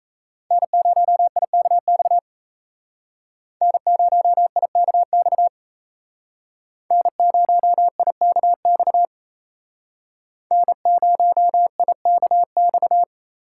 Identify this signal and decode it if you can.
Morse code, heard as a keyed tone, about 35 wpm, 700 Hz. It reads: N0IKX N0IKX N0IKX N0IKX
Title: Speed Racing - USA 1x3 Call Signs 28wpm